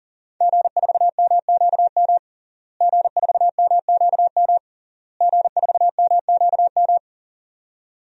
Morse code, heard as a keyed tone, about 40 wpm, 700 Hz. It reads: G4MQM G4MQM G4MQM
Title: DXCC-100 Call Signs - 5 Characters - Repeated 3x - 40wpm